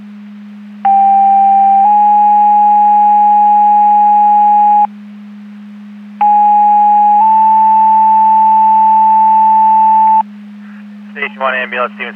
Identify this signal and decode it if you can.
Two-tone page, heard as a one-second tone, then a three-second tone, about 790 then 830 Hz, about 830 then 880 Hz, with speech. It is heard from 0.8 to 10.2 s.